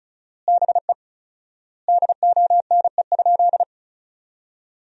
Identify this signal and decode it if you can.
Morse code, heard as a keyed tone, about 36 wpm, 700 Hz. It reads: BE DONE?
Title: 4x Word-Spacing - Sentences from Top 500 Words - 35wpm